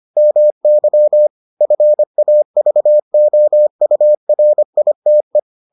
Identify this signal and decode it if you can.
Morse code, heard as a keyed tone, about 25 wpm, 600 Hz. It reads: MY FAVOURITE